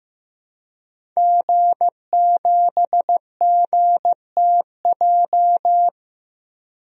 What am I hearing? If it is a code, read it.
Morse code, heard as a keyed tone, about 15 wpm, 700 Hz. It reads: G7GTJ
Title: DXCC-100 Call Signs - 5 Characters - 15wpm